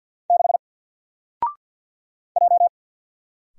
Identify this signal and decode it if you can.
Morse code, heard as a keyed tone, about 50 wpm, 700 Hz. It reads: 6 J
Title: Single Character (Letter:Number:Punctuation:Prosign) 50wpm